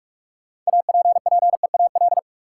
Morse code, heard as a keyed tone, about 41 wpm, 700 Hz. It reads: APPEAL
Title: Sentences from Top 1300 Words - New Word Drill - 45wpm